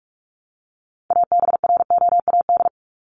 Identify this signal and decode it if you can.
Morse code, heard as a keyed tone, about 44 wpm, 700 Hz. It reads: ABROAD
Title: Top 1751-2000 Words 45wpm